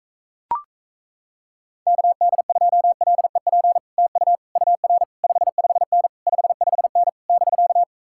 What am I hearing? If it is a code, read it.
Morse code, heard as a keyed tone, about 37 wpm, 700 Hz. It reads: KD2LEP TU UR 55N 55N <BK>